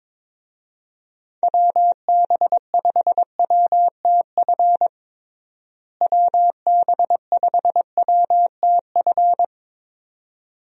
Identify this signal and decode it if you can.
Morse code, heard as a keyed tone, about 21 wpm, 700 Hz. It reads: WB5WTF WB5WTF
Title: USA Call Signs - 6 Characters - Repeated 2x - 22wpm